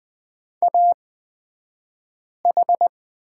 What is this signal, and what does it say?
Morse code, heard as a keyed tone, about 20 wpm, spaced as for 5 wpm, 700 Hz. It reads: AH
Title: 254 Lesson 41d - Period - Two Characters 20wpm